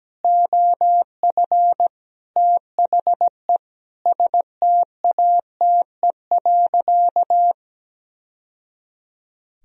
Morse code, heard as a keyed tone, about 17 wpm, 700 Hz. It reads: OF THE STATE.
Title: Sentences from Top 1000 Words - Review Part 4 - 17wpm